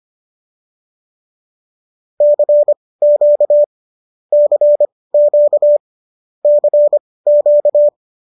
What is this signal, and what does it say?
Morse code, heard as a keyed tone, about 25 wpm, spaced as for 18 wpm, 590 Hz. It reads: CQ CQ CQ